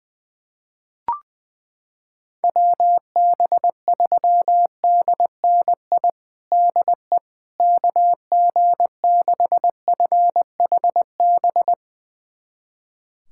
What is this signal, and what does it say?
Morse code, heard as a keyed tone, about 21 wpm, 700 Hz. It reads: WB3DNI DE KG6FHB
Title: USA Callsign DE Callsign - 6 Characters - 20wpm